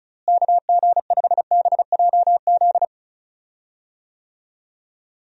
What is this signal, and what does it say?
Morse code, heard as a keyed tone, about 36 wpm, 700 Hz. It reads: KG5BJZ
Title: USA Call Signs - 6 Characters - Repeated 2x - 35wpm